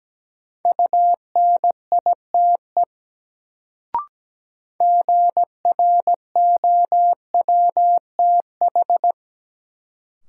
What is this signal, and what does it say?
Morse code, heard as a keyed tone, about 17 wpm, 700 Hz. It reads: UNITE GROWTH
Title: Sentences from Top 1000 Words - Review Part 5 - New Word Drill - 17wpm